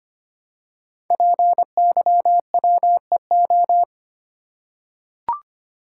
Morse code, heard as a keyed tone, about 25 wpm, 700 Hz. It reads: PYWEO
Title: Code Groups - 5 Letters - 25wpm